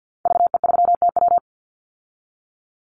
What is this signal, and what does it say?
Morse code, heard as a keyed tone, about 48 wpm, 700 Hz. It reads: VE3TW
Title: Call Signs Repeated 3x - 50wpm